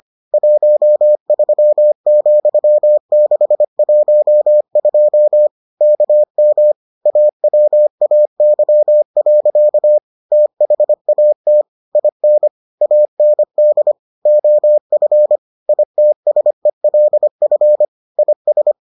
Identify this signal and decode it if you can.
Morse code, heard as a keyed tone, about 25 wpm, 600 Hz. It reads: 13,612 KM AWAY. THAT IN AND OF ITSELF IS